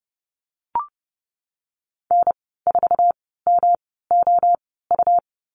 Morse code, heard as a keyed tone, about 30 wpm, spaced as for 17 wpm, 700 Hz. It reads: N4MOU
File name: Lesson 19g - Letter w - Call Signs 30wpm